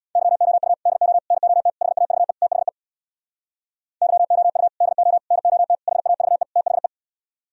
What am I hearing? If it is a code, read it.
Morse code, heard as a keyed tone, about 64 wpm, 700 Hz. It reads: YOU DO NOT HAVE THE YOU DO NOT HAVE THE